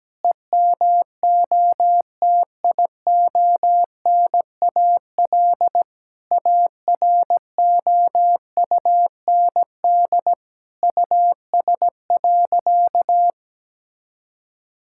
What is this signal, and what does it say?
Morse code, heard as a keyed tone, about 17 wpm, 700 Hz. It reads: EMOTIONAL AROUND US.
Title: Sentences from Top 1000 Words - Review Part 5 - 17wpm